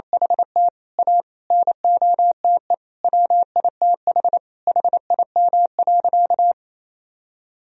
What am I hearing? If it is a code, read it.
Morse code, heard as a keyed tone, about 28 wpm, 700 Hz. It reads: HT A NOTE WITH HIM.
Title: Sentences from Top 1000 Words - Review Part 2 - 28wpm